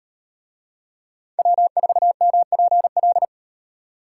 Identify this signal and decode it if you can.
Morse code, heard as a keyed tone, about 36 wpm, 700 Hz. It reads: W4MPL